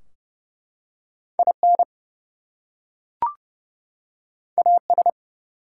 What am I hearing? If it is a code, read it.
Morse code, heard as a keyed tone, about 30 wpm, 700 Hz. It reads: IN AS
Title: Two Letter Words - 30wpm